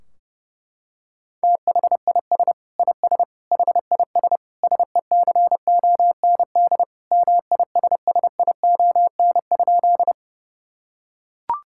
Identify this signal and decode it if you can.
Morse code, heard as a keyed tone, about 30 wpm, 700 Hz. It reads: THIS IS HIS SECOND MISSION?
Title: Sentences from Top 1600 Words 30wpm